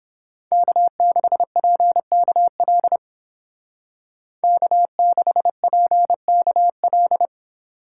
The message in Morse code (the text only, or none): K6PKL K6PKL